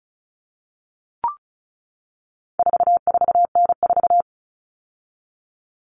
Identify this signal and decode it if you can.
Morse code, heard as a keyed tone, about 34 wpm, 700 Hz. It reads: 44N4
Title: All Cut Numbers - 4-Digit 35wpm